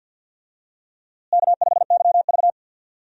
Morse code, heard as a keyed tone, about 48 wpm, 700 Hz. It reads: K5YV
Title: USA Call Signs - 4 Characters - Repeated 2x - 50wpm